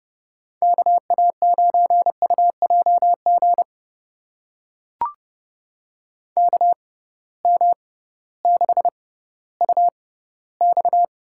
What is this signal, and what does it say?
Morse code, heard as a keyed tone, about 32 wpm, spaced as for 17 wpm, 700 Hz. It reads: KA9UJG KM6UX
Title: USA Call Signs - 6 Characters - 6x Character Spacing - 30wpm